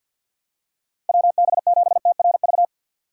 Morse code, heard as a keyed tone, about 51 wpm, 700 Hz. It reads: WB7TRV
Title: USA Call Signs - 6 Characters - 4x Character Spacing - 50wpm